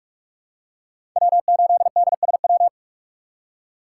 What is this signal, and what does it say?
Morse code, heard as a keyed tone, about 46 wpm, 700 Hz. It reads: W8DSW
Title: USA Call Signs - 4x Character Spacing - 45wpm